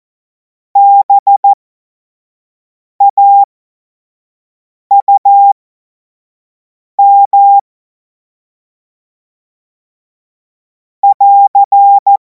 Morse code, heard as a keyed tone, about 14 wpm, spaced as for 5 wpm, 800 Hz. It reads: BAUM <AR>
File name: Morsekurs / Witze / Witz 24. März 2026 25 F70